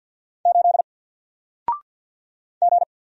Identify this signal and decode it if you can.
Morse code, heard as a keyed tone, about 48 wpm, 700 Hz. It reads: 8 G
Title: Single Character (Letter:Number:Punctuation:Prosign) 50wpm